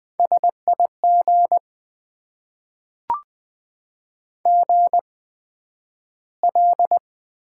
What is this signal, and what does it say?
Morse code, heard as a keyed tone, about 20 wpm, spaced as for 11 wpm, 700 Hz. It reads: SIG GL
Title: Top 90 QSO Elements - 8x Character Spacing - 20wpm